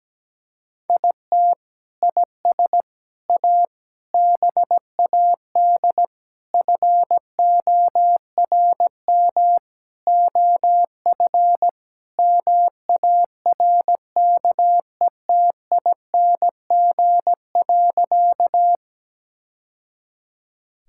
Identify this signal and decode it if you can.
Morse code, heard as a keyed tone, about 17 wpm, 700 Hz. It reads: IT IS A BAD FORM OF MARKETING.
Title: Sentences from Top 1000 Words - Review Part 5 - 17wpm